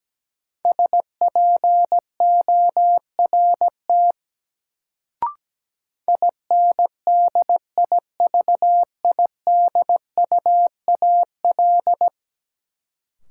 Morse code, heard as a keyed tone, about 17 wpm, 700 Hz. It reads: SPORT INDIVIDUAL
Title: Sentences from Top 1000 Words - Review Part 4 - New Word Drill - 17wpm